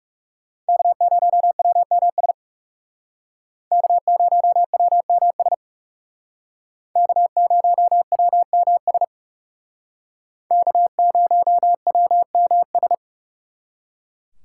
Morse code, heard as a keyed tone, about 38 wpm, 700 Hz. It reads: K0WMS K0WMS K0WMS K0WMS